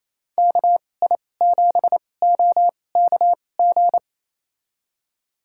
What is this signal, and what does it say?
Morse code, heard as a keyed tone, about 28 wpm, spaced as for 20 wpm, 700 Hz. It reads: KI7OKG
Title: USA Call Signs - 2x Character Spacing - 28wpm